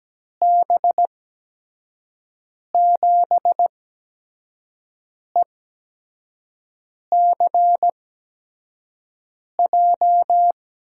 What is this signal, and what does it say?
Morse code, heard as a keyed tone, about 17 wpm, spaced as for 5 wpm, 700 Hz. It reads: B7ECJ
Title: USA Call Signs - 6 Characters - 8x Character Spacing - 17wpm